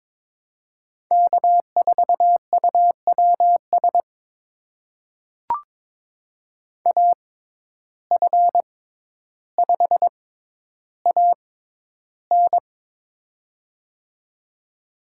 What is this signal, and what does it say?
Morse code, heard as a keyed tone, about 22 wpm, spaced as for 11 wpm, 700 Hz. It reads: K4UWS AF5AN
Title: USA Call Signs - 5 Characters - 6x Character Spacing - 22wpm